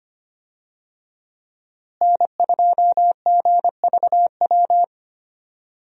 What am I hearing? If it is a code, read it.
Morse code, heard as a keyed tone, about 25 wpm, 700 Hz. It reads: N2GVW
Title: USA Call Signs - 5 Characters - Repeated 3x - 25wpm